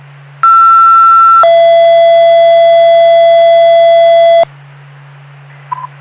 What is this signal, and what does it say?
Two-tone page, heard as a one-second tone, then a three-second tone, about 1340 then 670 Hz, with noise. It is heard from 0.4 to 4.4 s.